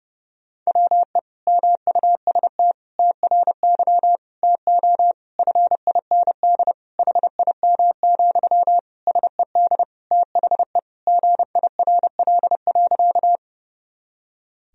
Morse code, heard as a keyed tone, about 30 wpm, 700 Hz. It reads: WE MUST TRY TO FIND HIM, SED THE GIRL.
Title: Sentences from Top 400 Words and Abbreviations 30wpm